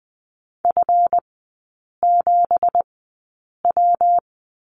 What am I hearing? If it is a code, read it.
Morse code, heard as a keyed tone, about 20 wpm, spaced as for 8 wpm, 700 Hz. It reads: F7W
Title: 166 Lesson 27e - Number 7 - Three Characters 20wpm